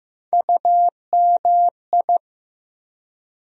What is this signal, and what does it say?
Morse code, heard as a keyed tone, about 15 wpm, 700 Hz. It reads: UMI
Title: USA Call Sign Postfixes - 3 Letters 15wpm